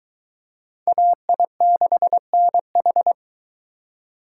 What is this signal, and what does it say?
Morse code, heard as a keyed tone, about 22 wpm, 700 Hz. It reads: AI6NH